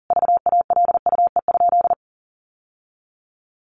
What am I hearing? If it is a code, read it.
Morse code, heard as a keyed tone, about 40 wpm, 700 Hz. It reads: VALUE?
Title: Sentences from Top 1400 Words 40wpm